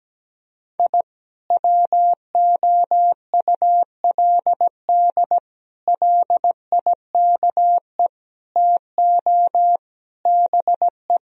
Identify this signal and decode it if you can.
Morse code, heard as a keyed tone, about 17 wpm, 700 Hz. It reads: I WOULD LIKE TO BE
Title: Sentences from Top 1000 Words - Review Part 3 - 17wpm